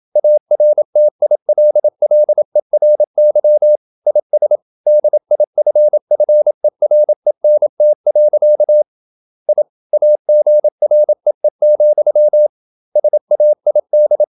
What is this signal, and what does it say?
Morse code, heard as a keyed tone, about 27 wpm, 600 Hz. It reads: ARTILLERY IS DIFFERENT. I AGREE, SAID
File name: Never 0027wpm